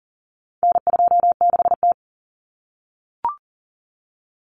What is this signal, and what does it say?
Morse code, heard as a keyed tone, about 40 wpm, 700 Hz. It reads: N26T
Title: Common Cut Numbers - 4-Digit 40wpm